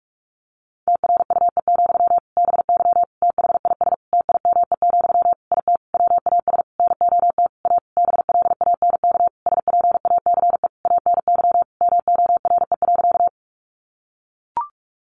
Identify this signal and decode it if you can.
Morse code, heard as a keyed tone, about 44 wpm, 700 Hz. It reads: TRUE, BY THIS TIME, IT WAS NOT A BLANK SPACE ANY MORE.
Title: Sentences from Top 1500 Words 45wpm